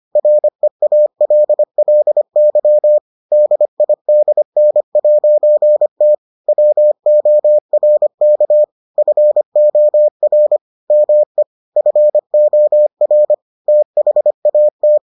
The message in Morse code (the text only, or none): REALLY DIDN'T WORK FOR ME FOR THAT